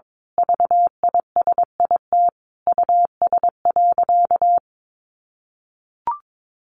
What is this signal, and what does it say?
Morse code, heard as a keyed tone, about 22 wpm, 700 Hz. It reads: VISIT US.
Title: Sentences from Top 1400 Words 22wpm